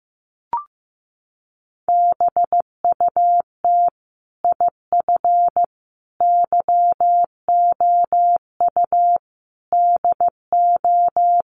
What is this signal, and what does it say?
Morse code, heard as a keyed tone, about 15 wpm, 700 Hz. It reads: BUT IF YOU DO
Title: Sets of 5 Words Encore 15wpm